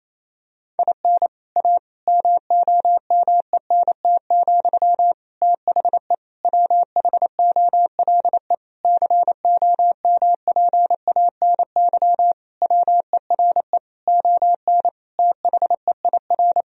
Morse code, heard as a keyed tone, about 28 wpm, 700 Hz. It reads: IN A MOMENT, THE WHOLE COMPANY WERE ON THEIR
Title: Sentences from Top 1000 Words - Review Part 3 - 28wpm